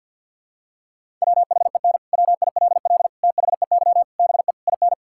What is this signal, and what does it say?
Morse code, heard as a keyed tone, about 50 wpm, 700 Hz. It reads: WHEN WILL THEY BE IN
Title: Sentences from Top 1000 Words - Review Part 6 - 50wpm